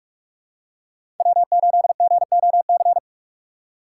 Morse code, heard as a keyed tone, about 41 wpm, 690 Hz. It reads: W8GOC
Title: USA Call Signs - 6x Character Spacing - 45wpm